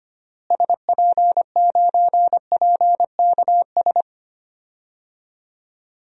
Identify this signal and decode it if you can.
Morse code, heard as a keyed tone, about 25 wpm, 700 Hz. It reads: SP9PKS
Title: DXCC-100 Call Signs - 6 Characters - Repeated 2x - 25wpm